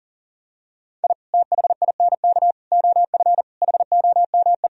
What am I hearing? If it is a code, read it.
Morse code, heard as a keyed tone, about 40 wpm, 700 Hz. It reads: I THINK OF HOME
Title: Sentences from Top 1000 Words - Review Part 3 - 40wpm